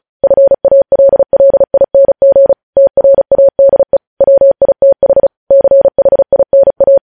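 Morse code, heard as a keyed tone, about 36 wpm, 550 Hz. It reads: FALLING TRADE WITH CHINA
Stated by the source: Morse code practice sending news headlines at 35 wpm.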